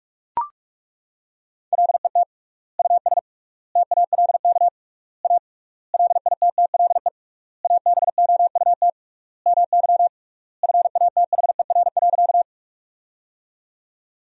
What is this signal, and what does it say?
Morse code, heard as a keyed tone, about 41 wpm, 700 Hz. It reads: LET US TALK A LITTLE ABOUT MY FATHER.
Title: 3x Word-Spacing - Sentences from Top 400 Words - 45wpm